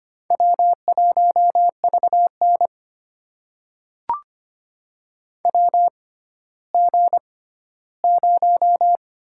Morse code, heard as a keyed tone, about 25 wpm, spaced as for 14 wpm, 700 Hz. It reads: W1VN WG0
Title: USA Call Signs - 4 Characters - 6x Character Spacing - 25wpm